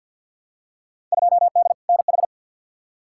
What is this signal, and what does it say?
Morse code, heard as a keyed tone, about 47 wpm, 700 Hz. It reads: 2D NH
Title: Field Day Exchanges Encore - 50wpm